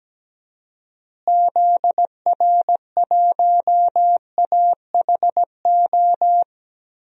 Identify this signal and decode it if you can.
Morse code, heard as a keyed tone, about 17 wpm, 700 Hz. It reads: ZR1AHO
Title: DXCC-100 Call Signs - 6 Characters - Repeated 2x - 17wpm